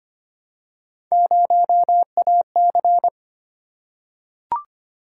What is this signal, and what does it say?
Morse code, heard as a keyed tone, about 25 wpm, 700 Hz. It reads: 0AC
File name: Three Letter-Number Combinations - 25wpm